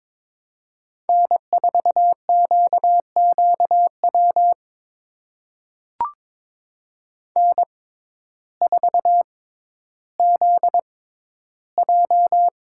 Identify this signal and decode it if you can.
Morse code, heard as a keyed tone, about 22 wpm, spaced as for 12 wpm, 700 Hz. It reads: N4QQW N4ZJ
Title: USA Call Signs - 6x Character Spacing - 22wpm